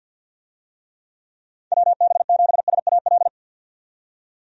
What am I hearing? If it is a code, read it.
Morse code, heard as a keyed tone, about 46 wpm, 700 Hz. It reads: WB7SAL
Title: USA Call Signs - 6 Characters - 50wpm